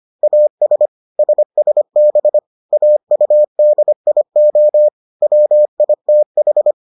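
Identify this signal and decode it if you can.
Morse code, heard as a keyed tone, about 25 wpm, 600 Hz. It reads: AS SSB AUDIO WITH
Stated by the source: Morse edition of the Foundations of Amateur Radio episodes.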